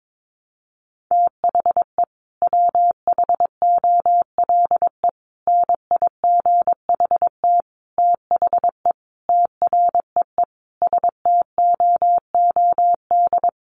Morse code, heard as a keyed tone, about 22 wpm, 700 Hz. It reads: THE WHOLE NIGHT THE TREE STOOD
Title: Sentences from Top 500 Words 22wpm